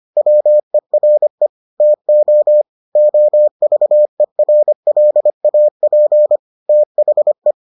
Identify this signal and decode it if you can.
Morse code, heard as a keyed tone, about 25 wpm, 600 Hz. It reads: WERE TO OVERLAP THE